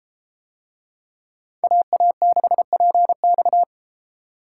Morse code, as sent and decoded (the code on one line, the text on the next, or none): .- .- -.... .--. -..-
AA6PX